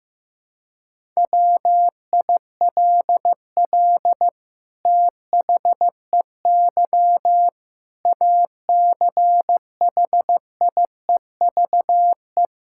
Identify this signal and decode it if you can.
Morse code, heard as a keyed tone, about 15 wpm, 700 Hz. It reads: WILL THEY ACHIEVE
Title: Sentences from Top 1000 Words - Review Part 6 - 15wpm